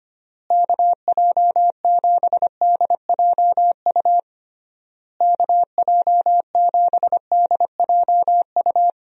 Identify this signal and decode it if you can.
Morse code, heard as a keyed tone, about 25 wpm, 700 Hz. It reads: KJ7DJU KJ7DJU
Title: USA Call Signs - 6 Characters - Repeated 3x - 25wpm